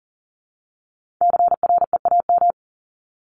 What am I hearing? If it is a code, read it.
Morse code, heard as a keyed tone, about 40 wpm, 700 Hz. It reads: CREAM
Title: Top 1251-1500 Words 40wpm